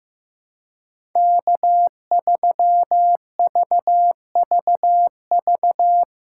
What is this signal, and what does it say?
Morse code, heard as a keyed tone, about 15 wpm, 700 Hz. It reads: K3VVV
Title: USA Call Signs - 5 Characters - 3x Character Spacing - 15wpm